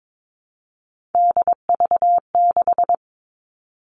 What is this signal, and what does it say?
Morse code, heard as a keyed tone, about 22 wpm, 700 Hz. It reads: DV6
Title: All Cut Numbers - 3-Digit 22wpm